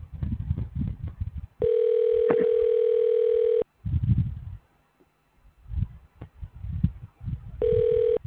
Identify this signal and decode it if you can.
Ringback tone (North America), 440 + 480 Hz.
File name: recordings / RE7f f a0dd71288ab7d9